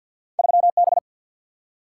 Morse code, heard as a keyed tone, about 52 wpm, 700 Hz. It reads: 3B
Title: Single-Letter Single-Number 50wpm